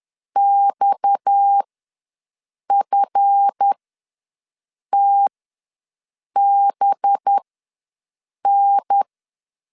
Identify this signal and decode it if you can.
Morse code, heard as a keyed tone, about 10 wpm, spaced as for 6 wpm, 800 Hz. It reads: XFTBN
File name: LEZIONI / 7 GF9Z / 7 GF ws 10wpm 3918